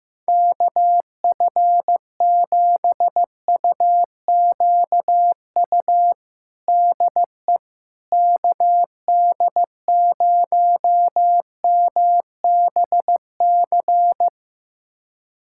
Morse code, heard as a keyed tone, about 15 wpm, 700 Hz. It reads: KF7UQU DE KD0MBC
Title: USA Callsign DE Callsign - 6 Characters - 15wpm